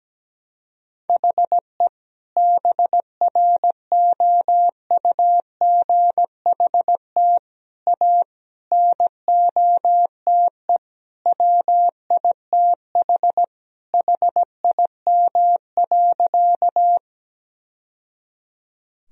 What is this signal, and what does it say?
Morse code, heard as a keyed tone, about 17 wpm, 700 Hz. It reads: HE BROUGHT A NOTE WITH HIM.
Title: Sentences from Top 1000 Words - Review Part 2 - 17wpm